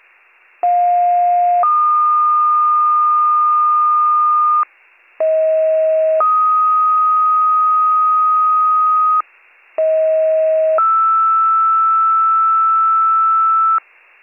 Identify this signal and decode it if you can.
Two-tone page, heard as a one-second tone, then a three-second tone, about 710 then 1220 Hz, about 640 then 1250 Hz, about 640 then 1320 Hz. It is heard from 0.6 to 13.8 s.